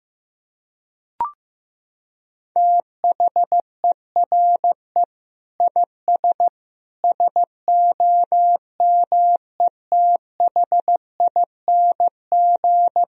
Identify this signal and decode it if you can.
Morse code, heard as a keyed tone, about 15 wpm, 700 Hz. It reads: THERE IS SOMETHING
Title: Sentences from Top 1000 Words - Review Part 1 - 15wpm